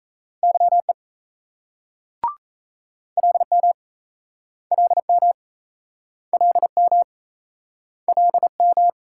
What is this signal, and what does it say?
Morse code, heard as a keyed tone, about 38 wpm, 700 Hz. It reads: YE LM LM LM LM